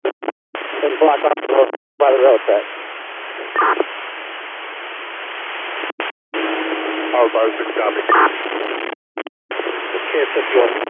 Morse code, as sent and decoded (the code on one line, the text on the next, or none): none